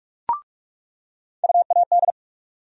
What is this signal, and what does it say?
Morse code, heard as a keyed tone, about 43 wpm, 700 Hz. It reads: UAD
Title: USA Call Sign Postfixes - 3 Letters 45wpm